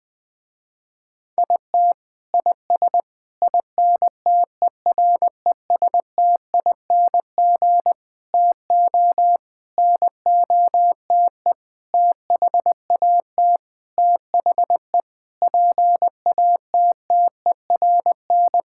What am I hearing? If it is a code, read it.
Morse code, heard as a keyed tone, about 20 wpm, 700 Hz. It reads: IT IS INTERESTING TO NOTE THAT THE PATTERN
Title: Sentences with 6-Word Phrases - Spoken Beginning - 20wpm